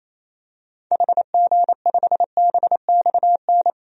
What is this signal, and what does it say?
Morse code, heard as a keyed tone, about 28 wpm, 700 Hz. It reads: HG5BXN
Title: DXCC-100 Call Signs - 6 Characters - 28wpm